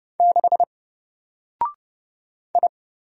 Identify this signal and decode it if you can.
Morse code, heard as a keyed tone, about 30 wpm, 700 Hz. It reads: 6 I
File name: Single-Letter-Number-30wpm